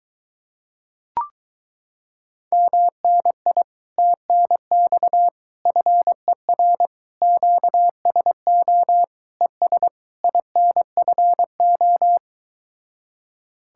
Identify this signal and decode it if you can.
Morse code, heard as a keyed tone, about 23 wpm, 700 Hz. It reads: MNI TNX FER QSO ES INFO